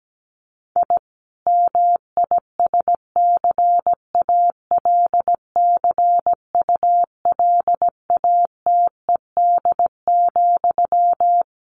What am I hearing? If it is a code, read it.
Morse code, heard as a keyed tone, about 17 wpm, 700 Hz. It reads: I MISCALCULATED,
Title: A Sound Of Thunder 17wpm